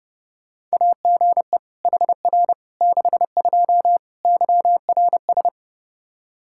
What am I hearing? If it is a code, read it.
Morse code, heard as a keyed tone, about 30 wpm, 700 Hz. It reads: AGE HR 62 YRS
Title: Practice QSO - Exchange 3 - More Info - 30wpm